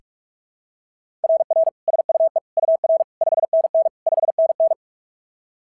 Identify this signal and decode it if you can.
Morse code, heard as a keyed tone, about 41 wpm, 690 Hz. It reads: RR SUE UR 5NN 5NN
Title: SOTA QSO Practice 45wpm